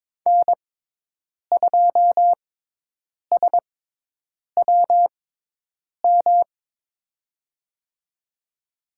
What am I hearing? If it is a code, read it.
Morse code, heard as a keyed tone, about 22 wpm, spaced as for 8 wpm, 700 Hz. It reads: N2SWM
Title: USA Call Signs - 6x Character Spacing - 22wpm